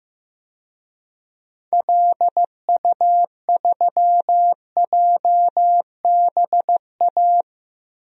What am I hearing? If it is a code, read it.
Morse code, heard as a keyed tone, about 15 wpm, 700 Hz. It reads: LU3JBA
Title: DXCC-100 Call Signs - 6 Characters - Repeated 2x - 15wpm